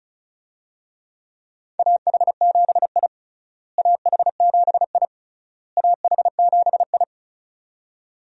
Morse code, heard as a keyed tone, about 34 wpm, 700 Hz. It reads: AH7I AH7I AH7I
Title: USA Call Signs - 4 Characters - Repeated 3x - 35wpm